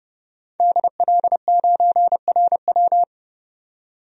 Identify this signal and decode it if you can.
Morse code, heard as a keyed tone, about 31 wpm, 700 Hz. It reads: DL9RW
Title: DXCC-100 Call Signs - 5 Characters - 30wpm